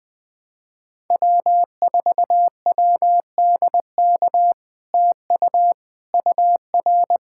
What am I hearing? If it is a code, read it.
Morse code, heard as a keyed tone, about 21 wpm, 700 Hz. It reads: W4WDK TU UR
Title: POTA QSO Practice - 20wpm